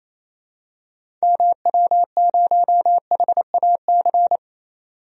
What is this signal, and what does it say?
Morse code, heard as a keyed tone, about 28 wpm, 700 Hz. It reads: MW0HAC
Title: DXCC-100 Call Signs - 6 Characters - 28wpm